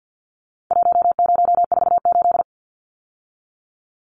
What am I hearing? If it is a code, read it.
Morse code, heard as a keyed tone, about 50 wpm, 700 Hz. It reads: 1048
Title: Numbers - Contest Style - 4-Digit 50wpm